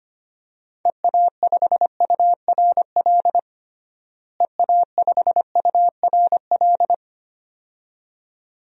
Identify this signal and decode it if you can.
Morse code, heard as a keyed tone, about 25 wpm, 700 Hz. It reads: EA5URL EA5URL
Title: DXCC-100 Call Signs - 6 Characters - Repeated 2x - 25wpm